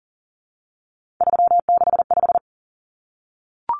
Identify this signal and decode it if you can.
Morse code, heard as a keyed tone, about 40 wpm, 700 Hz. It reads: 365
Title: Common Cut Numbers - 3-Digit 40wpm